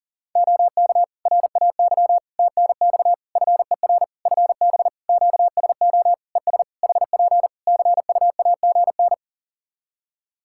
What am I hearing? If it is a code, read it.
Morse code, heard as a keyed tone, about 40 wpm, 700 Hz. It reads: OK RAY TNX FER FB QSO ES HP CUAGN
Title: Practice QSO - Exchange 4 - Ending - 40wpm